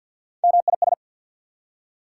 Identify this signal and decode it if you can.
Morse code, heard as a keyed tone, about 48 wpm, 690 Hz. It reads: MIS
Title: Prefixes and Suffixes 50wpm